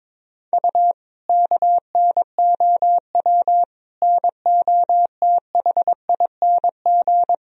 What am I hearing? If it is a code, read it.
Morse code, heard as a keyed tone, about 21 wpm, 700 Hz. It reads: U KNOW NOTHING
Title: Sentences from Top 500 Words and Abbreviations 22wpm